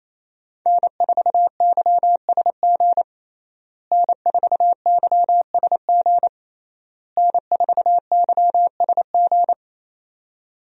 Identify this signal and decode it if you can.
Morse code, heard as a keyed tone, about 27 wpm, 700 Hz. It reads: N4YSG N4YSG N4YSG
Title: USA Call Signs - 5 Characters - Repeated 3x - 28wpm